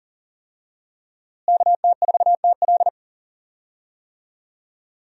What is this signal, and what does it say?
Morse code, heard as a keyed tone, about 40 wpm, 700 Hz. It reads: KT4TL
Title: USA Call Signs - 5 Characters - Repeated 2x - 40wpm